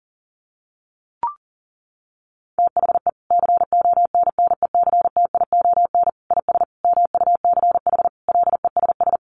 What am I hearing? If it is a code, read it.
Morse code, heard as a keyed tone, about 40 wpm, 700 Hz. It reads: THE CONNECTION IS MUCH LESS
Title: Sentences from Top 1300 Words 40wpm